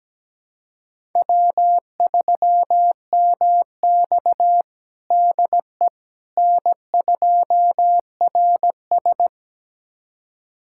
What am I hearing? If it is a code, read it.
Morse code, heard as a keyed tone, about 17 wpm, 700 Hz. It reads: W3MX DE N2RS
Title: USA Callsign DE Callsign - 4 Characters - 17wpm